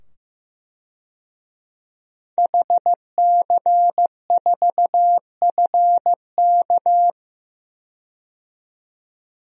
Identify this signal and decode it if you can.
Morse code, heard as a keyed tone, about 15 wpm, 700 Hz. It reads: HC4FK
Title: DXCC-100 Call Signs - 5 Characters - 15wpm